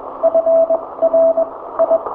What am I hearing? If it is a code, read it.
Morse code, heard as a keyed tone, about 22 wpm, spaced as for 16 wpm, 660 Hz. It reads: FRI
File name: hamradio / last 500khz cw